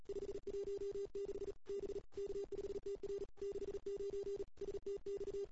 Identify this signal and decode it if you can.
Morse code, heard as a keyed tone, about 38 wpm, 400 Hz. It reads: 516 B K5TR 69 STX
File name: CW / 2008-ss-cw-k5tr-op-n5rz -02:00